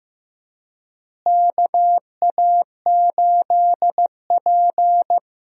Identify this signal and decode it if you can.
Morse code, heard as a keyed tone, about 15 wpm, 700 Hz. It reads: KA8P
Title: USA Call Signs - 4 Characters - Repeated 2x - 15wpm